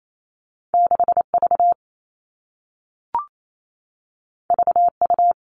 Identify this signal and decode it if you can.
Morse code, heard as a keyed tone, about 28 wpm, 700 Hz. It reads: 6V VU
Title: All Cut Numbers - 2-Digit 28wpm